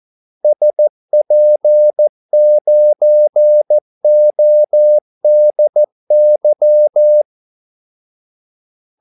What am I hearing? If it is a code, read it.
Morse code, heard as a keyed tone, about 14 wpm, 600 Hz. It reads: SP9ODY
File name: Callsigns-14wpm